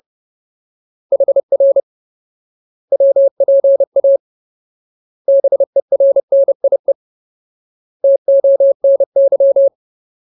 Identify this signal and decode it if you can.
Morse code, heard as a keyed tone, about 30 wpm, 550 Hz. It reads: HR WPA BERNIE TONY
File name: Ultimate Word List / 30WPM Ultimate